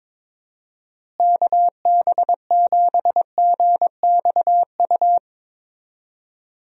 Morse code, heard as a keyed tone, about 22 wpm, 700 Hz. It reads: KB7GXU
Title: USA Call Signs - 6 Characters - Repeated 3x - 22wpm